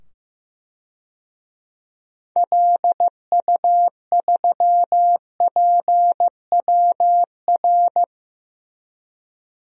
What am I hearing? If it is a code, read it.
Morse code, heard as a keyed tone, about 15 wpm, 700 Hz. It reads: LU3PWR
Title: DXCC-100 Call Signs - 6 Characters - 15wpm